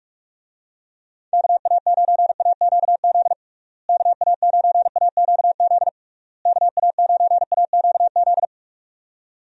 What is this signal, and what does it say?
Morse code, heard as a keyed tone, about 46 wpm, 700 Hz. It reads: KA9AQZ KA9AQZ KA9AQZ
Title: USA Call Signs - 6 Characters - Repeated 3x - 45wpm